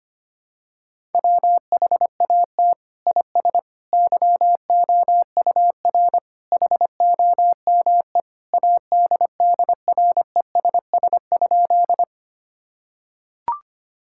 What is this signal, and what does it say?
Morse code, heard as a keyed tone, about 25 wpm, 700 Hz. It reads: WHAT IS YOUR HOME ADDRESS?
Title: Sentences from Top 1000 Words - Review Part 6 - 25wpm